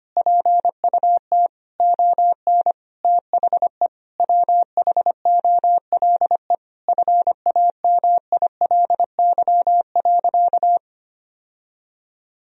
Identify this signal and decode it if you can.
Morse code, heard as a keyed tone, about 25 wpm, 700 Hz. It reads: PUT ON THE WHOLE FAMILY.
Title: Sentences from Top 1000 Words - Review Part 1 - 25wpm